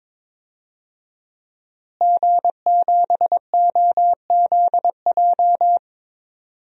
Morse code, heard as a keyed tone, about 21 wpm, 700 Hz. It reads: G7OZJ
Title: DXCC-100 Call Signs - 5 Characters - Repeated 3x - 22wpm